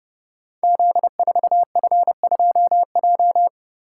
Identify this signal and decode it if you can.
Morse code, heard as a keyed tone, about 30 wpm, 700 Hz. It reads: Z4F2J
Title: Code Groups - 5 Letter:Numbers - 30wpm